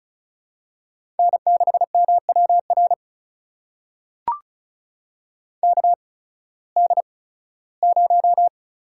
Morse code, heard as a keyed tone, about 36 wpm, spaced as for 19 wpm, 690 Hz. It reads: N6MWR KD0
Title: USA Call Signs - 5 Characters - 8x Character Spacing - 35wpm